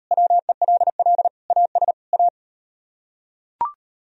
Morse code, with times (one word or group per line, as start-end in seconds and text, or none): 0.1-1.3 WELL
1.5-1.9 AS
2.1-2.3 A